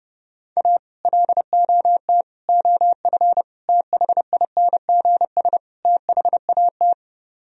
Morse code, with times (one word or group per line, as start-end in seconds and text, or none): 0.6-0.8 A
1.1-2.2 LOT
2.5-3.4 OF
3.7-5.6 THINGS
5.8-6.9 THAT